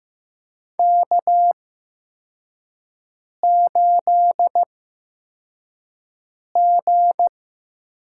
Morse code, heard as a keyed tone, about 15 wpm, spaced as for 4 wpm, 700 Hz. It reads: K8G
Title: USA Call Signs - 4 Characters - 8x Character Spacing - 15wpm